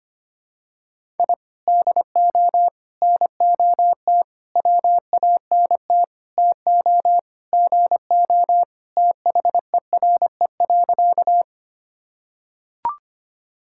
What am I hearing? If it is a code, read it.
Morse code, heard as a keyed tone, about 25 wpm, 700 Hz. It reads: I DO NOT WANT TO GO THERE.
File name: practice / 25wpm / Sentences with 4-Word Phrases - 25wpm